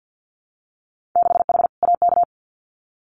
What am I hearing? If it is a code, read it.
Morse code, heard as a keyed tone, about 50 wpm, 700 Hz. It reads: 6H AK
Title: Winter Field Day Exchanges Encore 50wpm